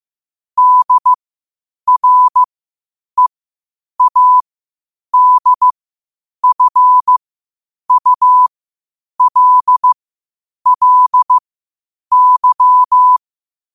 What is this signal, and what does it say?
Morse code, heard as a keyed tone, about 15 wpm, spaced as for 8 wpm, 1000 Hz. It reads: DREADFULLY
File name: qotdfiles / 8wpm